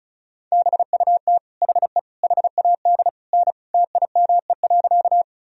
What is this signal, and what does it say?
Morse code, heard as a keyed tone, about 34 wpm, 700 Hz. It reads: BUT HE HAD N TIME.
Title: Sentences from Top 100 Words and Abbreviations 35wpm